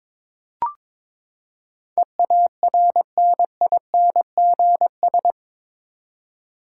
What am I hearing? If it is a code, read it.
Morse code, heard as a keyed tone, about 22 wpm, 700 Hz. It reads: EARNINGS
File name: practice / 22wpm / Eight Letter Words - 22wpm